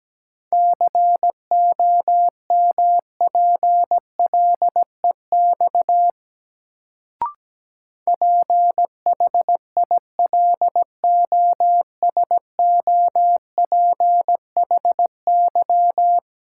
Morse code, heard as a keyed tone, about 17 wpm, 700 Hz. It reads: COMPLEX PHILOSOPHY
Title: Sentences from Top 900 Words - New Word Drill - 17wpm